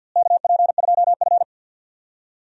Morse code, heard as a keyed tone, about 46 wpm, 700 Hz. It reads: KP2F
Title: DXCC-100 Call Signs - 4 Characters - Repeated 3x - 50wpm